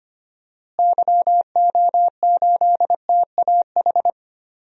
Morse code, heard as a keyed tone, about 25 wpm, 700 Hz. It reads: YO8TAH
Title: DXCC-100 Call Signs - 6 Characters - Repeated 2x - 25wpm